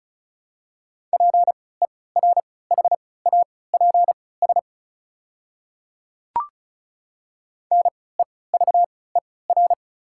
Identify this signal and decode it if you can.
Morse code, heard as a keyed tone, about 34 wpm, spaced as for 20 wpm, 690 Hz. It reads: PERHAPS NEVER
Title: 3x Character Spacing - Top 500 Words - 35wpm